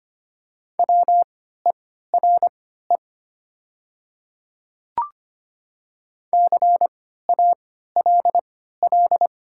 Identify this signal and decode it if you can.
Morse code, heard as a keyed tone, about 25 wpm, spaced as for 14 wpm, 700 Hz. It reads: WERE CALL
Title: 3x Character Spacing - Top 100 Words - 25wpm